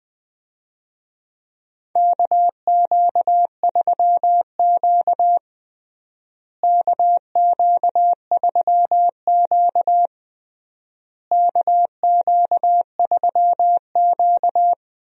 Morse code, heard as a keyed tone, about 20 wpm, 700 Hz. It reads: KQ3Q KQ3Q KQ3Q
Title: USA Call Signs - 4 Characters - Repeated 3x - 20wpm